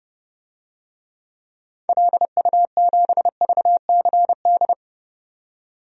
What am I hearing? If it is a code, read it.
Morse code, heard as a keyed tone, about 30 wpm, 700 Hz. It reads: LU7VCD
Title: DXCC-100 Call Signs - 6 Characters - 30wpm